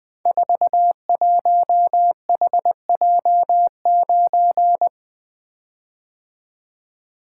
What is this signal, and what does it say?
Morse code, heard as a keyed tone, about 21 wpm, 700 Hz. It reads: 41HJ9
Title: Code Groups - 5 Letter:Numbers - 20wpm